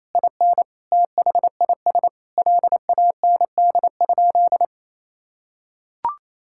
Morse code, heard as a keyed tone, about 28 wpm, 700 Hz. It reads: IN THIS LAND?
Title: Sentences from Top 1600 Words 28wpm